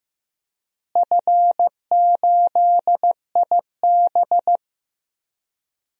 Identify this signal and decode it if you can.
Morse code, heard as a keyed tone, about 15 wpm, 700 Hz. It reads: F8IB
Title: DXCC-100 Call Signs - 4 Characters - Repeated 3x - 15wpm